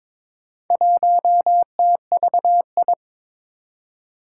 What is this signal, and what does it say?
Morse code, heard as a keyed tone, about 23 wpm, 700 Hz. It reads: 1TVI
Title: Four Challenging Letter-Number Combinations 22wpm